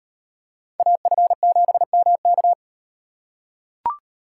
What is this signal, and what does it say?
Morse code, heard as a keyed tone, about 38 wpm, 700 Hz. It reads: AF7MK